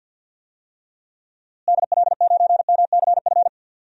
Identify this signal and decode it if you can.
Morse code, heard as a keyed tone, about 46 wpm, 700 Hz. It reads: DL9MCF
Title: DXCC-100 Call Signs - 6 Characters - Repeated 3x - 50wpm